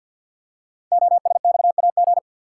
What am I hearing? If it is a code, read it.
Morse code, heard as a keyed tone, about 50 wpm, 700 Hz. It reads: OSXAG
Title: Code Groups - 5 Letters - 50wpm